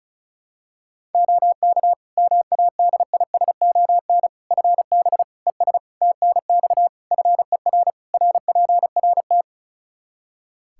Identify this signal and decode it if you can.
Morse code, heard as a keyed tone, about 34 wpm, 700 Hz. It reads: OK MADISON FB ES TNX FER RPRT
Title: Practice QSO - Exchange 2 - Info - 35wpm